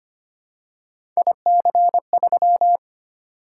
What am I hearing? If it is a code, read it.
Morse code, heard as a keyed tone, about 25 wpm, 700 Hz. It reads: IC3
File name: Three Letter-Number Combinations - 25wpm